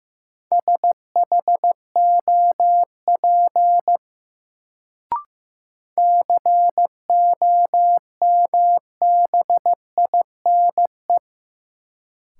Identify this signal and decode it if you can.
Morse code, heard as a keyed tone, about 15 wpm, 700 Hz. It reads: SHOP COMBINE
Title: Sentences from Top 1600 Words - New Word Drill - 15wpm